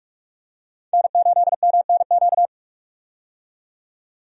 Morse code, heard as a keyed tone, about 46 wpm, 700 Hz. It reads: N8MNQ
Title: USA Call Signs - 5 Characters - 45wpm